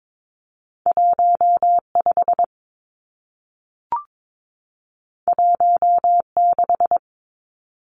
Morse code, heard as a keyed tone, about 22 wpm, 700 Hz. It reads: 15 16
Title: Numbers - 2-Digit 22wpm